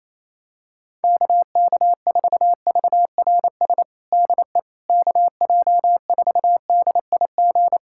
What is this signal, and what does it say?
Morse code, heard as a keyed tone, about 28 wpm, 700 Hz. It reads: KK4VRS DE KJ4DIG
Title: USA Callsign DE Callsign - 6 Characters - 28wpm